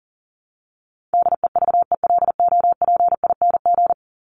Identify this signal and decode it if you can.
Morse code, heard as a keyed tone, about 40 wpm, 700 Hz. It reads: DEVELOPING
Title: ING Suffix - Top 100 Words with Base Words 40wpm